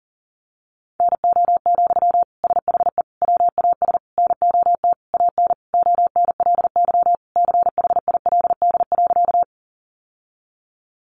Morse code, heard as a keyed tone, about 40 wpm, 700 Hz. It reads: NO, SHE WAS NOT AN ONLY CHILD.
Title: Sentences from Top 800 Words 40wpm